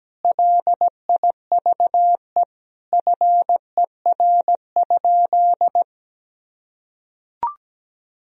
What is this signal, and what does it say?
Morse code, heard as a keyed tone, about 17 wpm, 700 Hz. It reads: LIVE FER?
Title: Sentences from Top 200 Words and Abbreviations 17wpm